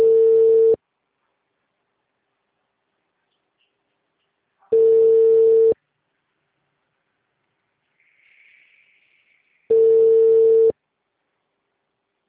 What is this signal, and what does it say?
Ringback tone (China, 450 Hz).